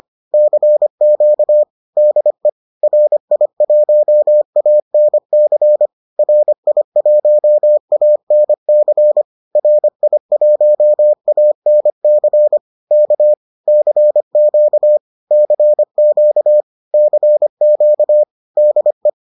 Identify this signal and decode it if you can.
Morse code, heard as a keyed tone, about 25 wpm, 600 Hz. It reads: CQ DE RI1ANC RI1ANC RI1ANC K CQ CQ CQ DE
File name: 25wpm-6